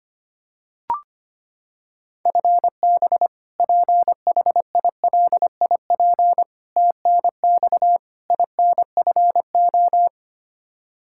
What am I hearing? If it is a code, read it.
Morse code, heard as a keyed tone, about 25 wpm, 700 Hz. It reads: FB PHILIP TNX INFO
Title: Practice QSO - Exchange 3 - More Info - 25wpm